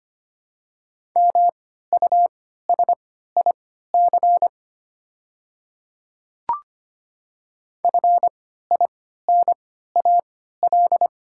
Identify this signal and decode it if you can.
Morse code, heard as a keyed tone, about 25 wpm, spaced as for 14 wpm, 700 Hz. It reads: MUSIC FINAL
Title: 3x Character Spacing - Top 500 Words - 25wpm